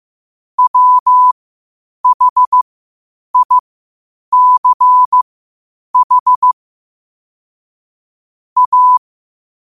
Morse code, heard as a keyed tone, about 15 wpm, spaced as for 8 wpm, 1000 Hz. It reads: WHICH A
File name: qotdfiles / 8wpm